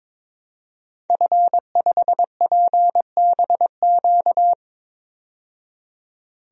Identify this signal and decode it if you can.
Morse code, heard as a keyed tone, about 22 wpm, 700 Hz. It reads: F5PBQ
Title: DXCC-100 Call Signs - 5 Characters - 22wpm